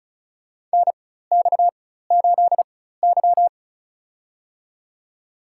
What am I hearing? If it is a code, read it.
Morse code, heard as a keyed tone, about 35 wpm, spaced as for 16 wpm, 700 Hz. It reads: NX8Y
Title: USA Call Signs - 4 Characters - 4x Character Spacing - 35wpm